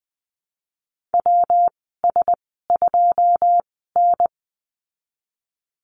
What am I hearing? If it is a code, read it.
Morse code, heard as a keyed tone, about 20 wpm, spaced as for 14 wpm, 700 Hz. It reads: WS2N
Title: 161 Lesson 26g - Letter g - Call Signs 20wpm